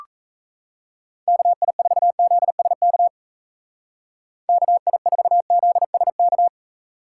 Morse code, heard as a keyed tone, about 37 wpm, 700 Hz. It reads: KI4ZSK KI4ZSK